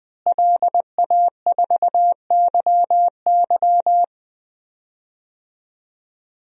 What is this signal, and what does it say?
Morse code, heard as a keyed tone, about 21 wpm, 700 Hz. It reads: LA4YY
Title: DXCC-100 Call Signs - 5 Characters - Repeated 2x - 20wpm